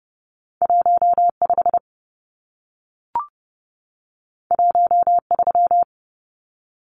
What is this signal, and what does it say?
Morse code, heard as a keyed tone, about 30 wpm, 700 Hz. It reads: 15 13
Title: Numbers - 2-Digit 30wpm